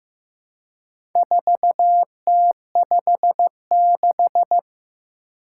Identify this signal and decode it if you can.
Morse code, heard as a keyed tone, about 15 wpm, 700 Hz. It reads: 4T56
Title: Four Challenging Letter-Number Combinations 15wpm